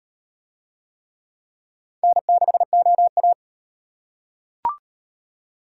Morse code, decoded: N6OA

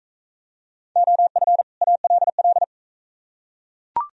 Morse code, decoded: OF ALL

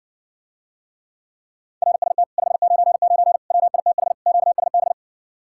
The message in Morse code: RST 599 WITH QSB

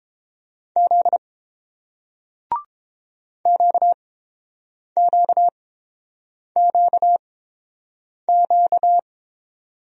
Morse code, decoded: Z QQQQ

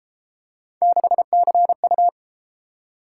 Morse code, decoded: 6CU

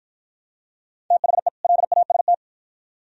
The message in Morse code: THE LAST

